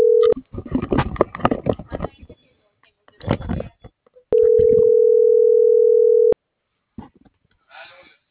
Ringback tone (North America), 440 + 480 Hz.